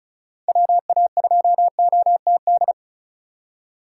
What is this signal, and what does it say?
Morse code, heard as a keyed tone, about 36 wpm, 700 Hz. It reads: WA2OTD
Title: USA Call Signs - 4x Character Spacing - 35wpm